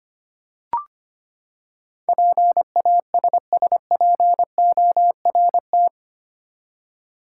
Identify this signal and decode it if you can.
Morse code, heard as a keyed tone, about 25 wpm, 700 Hz. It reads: PASSPORT
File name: Eight Letter Words - 25wpm